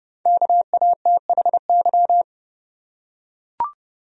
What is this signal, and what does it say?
Morse code, heard as a keyed tone, about 30 wpm, 700 Hz. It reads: KATHY
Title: USA Common Names - 30wpm